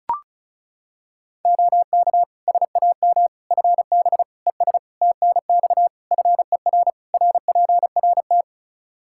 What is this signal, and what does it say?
Morse code, heard as a keyed tone, about 36 wpm, 700 Hz. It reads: OK SAM FB ES TNX FER RPRT
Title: Practice QSO - Exchange 2 - Info - 35wpm